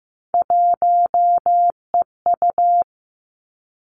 Morse code, heard as a keyed tone, about 15 wpm, 700 Hz. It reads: 1EU
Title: Three Challenging Letter-Number Combinations 15wpm